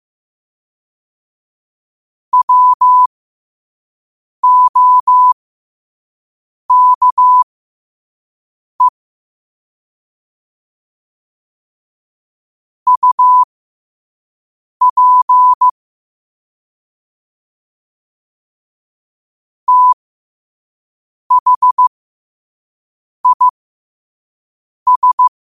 Morse code, heard as a keyed tone, about 15 wpm, spaced as for 5 wpm, 1000 Hz. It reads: WOKE UP THIS
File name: 5wpm